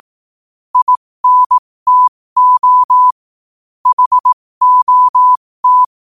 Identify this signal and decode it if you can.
Morse code, heard as a keyed tone, about 18 wpm, spaced as for 15 wpm, 1000 Hz. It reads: INTO HOT
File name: qotdfiles / 15wpm